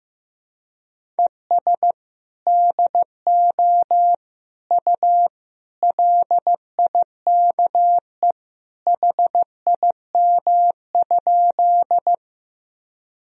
Morse code, heard as a keyed tone, about 16 wpm, 700 Hz. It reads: ES DO U LIKE HIM?
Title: Sentences from Top 100 Words and Abbreviations 15wpm